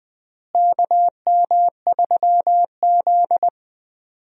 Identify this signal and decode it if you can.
Morse code, heard as a keyed tone, about 21 wpm, 700 Hz. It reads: KM3Z
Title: USA Call Signs - 4 Characters - Repeated 2x - 20wpm